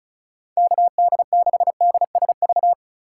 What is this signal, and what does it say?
Morse code, heard as a keyed tone, about 36 wpm, 700 Hz. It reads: KD6DSV
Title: USA Call Signs - 2x Character Spacing - 35wpm